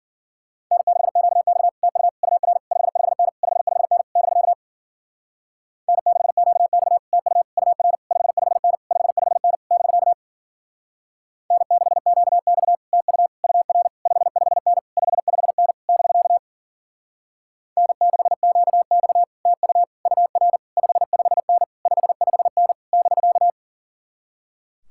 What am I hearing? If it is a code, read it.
Morse code, heard as a keyed tone, about 51 wpm, 700 Hz. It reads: N6QX TU UR 55N 55N <BK> N6QX TU UR 55N 55N <BK> N6QX TU UR 55N 55N <BK> N6QX TU UR 55N 55N <BK>